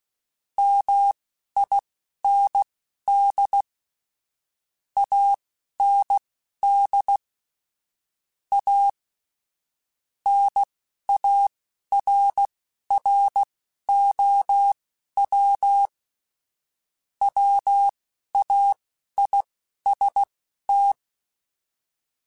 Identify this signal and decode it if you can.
Morse code, heard as a keyed tone, about 16 wpm, spaced as for 11 wpm, 800 Hz. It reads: MIND AND A NARROW WAIST
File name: Chuckles / Chuckles-mp3s / MiddleAge @10